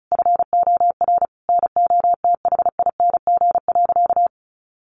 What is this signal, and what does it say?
Morse code, heard as a keyed tone, about 34 wpm, 700 Hz. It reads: FOR NOTHING.
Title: Sentences from Top 1400 Words 35wpm